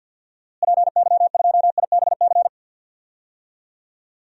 Morse code, heard as a keyed tone, about 52 wpm, 700 Hz. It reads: PY2IBC
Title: DXCC-100 Call Signs - 6 Characters - Repeated 3x - 50wpm